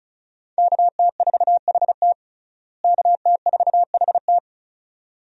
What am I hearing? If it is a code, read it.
Morse code, heard as a keyed tone, about 34 wpm, 700 Hz. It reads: KT4HT KT4HT
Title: USA Call Signs - 5 Characters - Repeated 2x - 35wpm